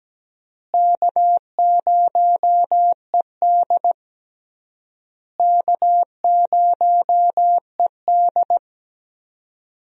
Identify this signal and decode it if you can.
Morse code, heard as a keyed tone, about 17 wpm, 700 Hz. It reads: K0ED K0ED
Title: USA Call Signs - 4 Characters - Repeated 3x - 17wpm